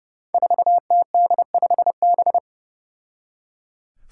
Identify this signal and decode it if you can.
Morse code, heard as a keyed tone, about 30 wpm, 700 Hz. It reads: 4TD5B
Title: Five Challenging Letter-Number Combinations 30wpm